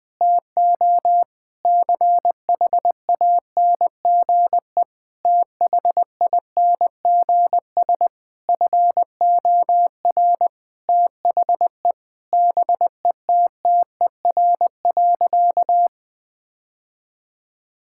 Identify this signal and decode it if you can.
Morse code, heard as a keyed tone, about 21 wpm, 700 Hz. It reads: TO CHANGE THINGS FOR THE BETTER.
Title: Difficult Sentences with Hints - 20wpm